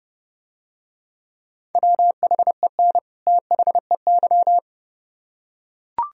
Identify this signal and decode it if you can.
Morse code, heard as a keyed tone, about 28 wpm, 700 Hz. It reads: WHEN THEY